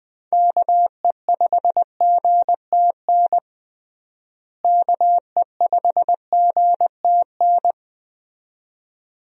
Morse code, heard as a keyed tone, about 20 wpm, 700 Hz. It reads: KE5GTN KE5GTN
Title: USA Call Signs - 6 Characters - Repeated 2x - 20wpm